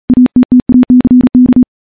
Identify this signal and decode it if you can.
Morse code, heard as a keyed tone, about 40 wpm, 250 Hz. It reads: ATTACK
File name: morse / morse-training-40WPM